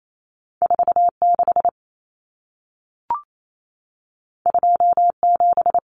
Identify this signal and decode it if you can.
Morse code, heard as a keyed tone, about 28 wpm, 700 Hz. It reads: 46 27
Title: Numbers - 2-Digit 28wpm